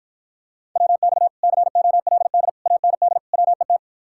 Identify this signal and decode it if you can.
Morse code, heard as a keyed tone, about 54 wpm, 700 Hz. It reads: WX COLD AND WET